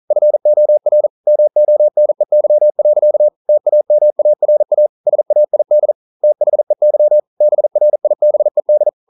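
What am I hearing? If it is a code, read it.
Morse code, heard as a keyed tone, about 41 wpm, 600 Hz. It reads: FOR MONEY. TAMARA SAID THEY BRIBED
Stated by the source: Morse code at 40 wpm, practice text, CW med Gnister.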